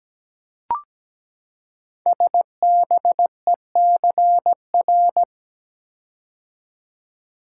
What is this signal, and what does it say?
Morse code, heard as a keyed tone, about 17 wpm, 700 Hz. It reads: SBECR
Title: Code Groups - 5 Letters - 17wpm